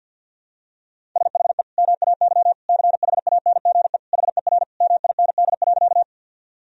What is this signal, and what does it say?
Morse code, heard as a keyed tone, about 50 wpm, 700 Hz. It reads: SHE MAY CHANGE HER MIND.
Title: Sentences from Top 1000 Words - Review Part 1 - 50wpm